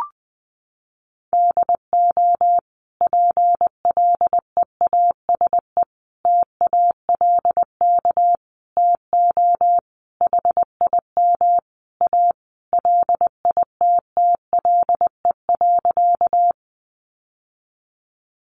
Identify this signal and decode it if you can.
Morse code, heard as a keyed tone, about 20 wpm, 700 Hz. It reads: DO PLEASE TALK TO HIM A LITTLE.
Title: Sentences from Top 1500 Words 20wpm